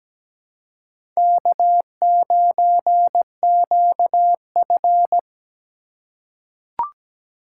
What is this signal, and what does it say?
Morse code, heard as a keyed tone, about 17 wpm, 700 Hz. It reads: K9QF
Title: USA Call Signs - 4 Characters - 3x Character Spacing - 17wpm